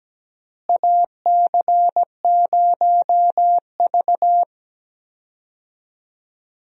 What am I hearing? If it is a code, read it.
Morse code, heard as a keyed tone, about 17 wpm, 700 Hz. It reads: AC0V
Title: USA Call Signs - 4 Characters - Repeated 3x - 17wpm